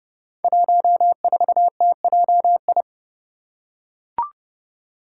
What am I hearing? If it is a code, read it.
Morse code, heard as a keyed tone, about 30 wpm, 700 Hz. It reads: 14TJI
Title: Five Challenging Letter-Number Combinations 30wpm